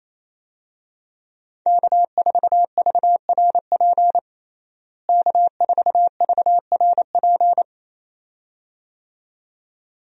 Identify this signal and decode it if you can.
Morse code, heard as a keyed tone, about 28 wpm, 700 Hz. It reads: K4VRP K4VRP
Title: USA Call Signs - 5 Characters - Repeated 2x - 28wpm